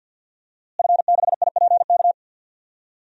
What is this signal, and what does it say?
Morse code, heard as a keyed tone, about 47 wpm, 700 Hz. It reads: F6IPK
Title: DXCC-100 Call Signs - 5 Characters - Repeated 2x - 50wpm